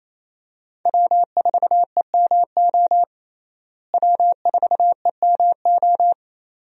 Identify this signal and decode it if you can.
Morse code, heard as a keyed tone, about 28 wpm, 700 Hz. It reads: W4EMO W4EMO
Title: USA Call Signs - 5 Characters - Repeated 2x - 28wpm